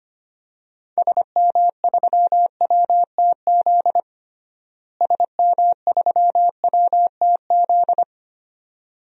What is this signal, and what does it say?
Morse code, heard as a keyed tone, about 25 wpm, 700 Hz. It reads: SM3WTZ SM3WTZ
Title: DXCC-100 Call Signs - 6 Characters - Repeated 2x - 25wpm